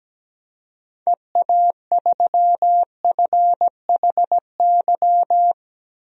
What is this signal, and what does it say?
Morse code, heard as a keyed tone, about 17 wpm, 700 Hz. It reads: EA3FHY
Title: DXCC-100 Call Signs - 6 Characters - 17wpm